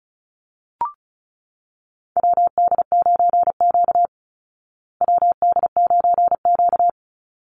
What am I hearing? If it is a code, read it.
Morse code, heard as a keyed tone, about 34 wpm, 700 Hz. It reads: WD9Q WD9Q
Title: Call Signs Repeated 3x - 35wpm